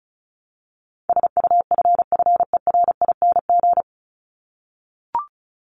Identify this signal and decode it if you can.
Morse code, heard as a keyed tone, about 34 wpm, 700 Hz. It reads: SUFFERING
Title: ING Suffix - Top 100 Words with Base Words 35wpm